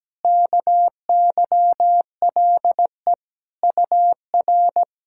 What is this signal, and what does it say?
Morse code, heard as a keyed tone, about 17 wpm, 700 Hz. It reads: KYLE UR
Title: SOTA QSO Practice 17wpm